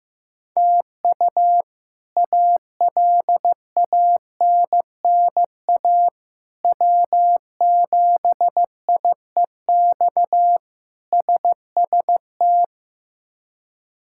Morse code, heard as a keyed tone, about 15 wpm, 700 Hz. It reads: TU ALANNA W7IEX SST
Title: K1USN SST QSO Practice - 15wpm